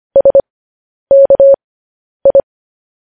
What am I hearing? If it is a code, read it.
Morse code, heard as a keyed tone, about 25 wpm, spaced as for 10 wpm, 550 Hz. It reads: SKI